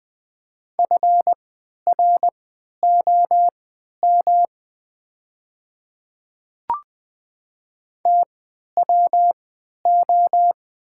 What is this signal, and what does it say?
Morse code, heard as a keyed tone, about 20 wpm, spaced as for 11 wpm, 700 Hz. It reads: FROM TWO
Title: 3x Character Spacing - Top 100 Words - 20wpm